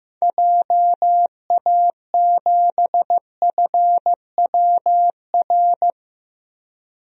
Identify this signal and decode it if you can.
Morse code, heard as a keyed tone, about 15 wpm, 700 Hz. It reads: JA7FWR
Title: DXCC-100 Call Signs - 6 Characters - 15wpm